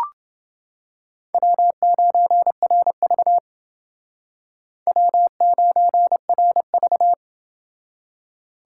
Morse code, heard as a keyed tone, about 28 wpm, 700 Hz. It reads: W9RV W9RV